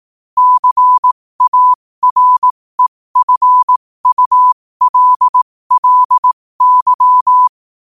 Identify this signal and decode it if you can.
Morse code, heard as a keyed tone, about 18 wpm, spaced as for 15 wpm, 1000 Hz. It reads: CAREFULLY
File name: qotdfiles / 15wpm